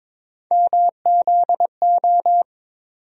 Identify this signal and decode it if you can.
Morse code, heard as a keyed tone, about 22 wpm, 700 Hz. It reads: MZO
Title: USA Call Sign Postfixes - 3 Letters 22wpm